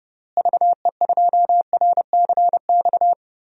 Morse code, heard as a keyed tone, about 32 wpm, 700 Hz. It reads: VE2RCX
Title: DXCC-100 Call Signs - 6 Characters - Repeated 3x - 30wpm